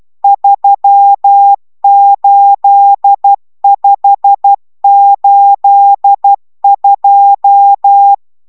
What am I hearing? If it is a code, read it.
Morse code, heard as a keyed tone, about 12 wpm, 800 Hz. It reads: 38582